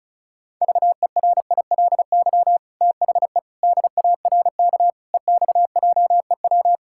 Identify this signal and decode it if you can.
Morse code, heard as a keyed tone, about 36 wpm, 700 Hz. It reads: VERILY THE DARK EXJEW
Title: Pangram Challenge v2 - 35wpm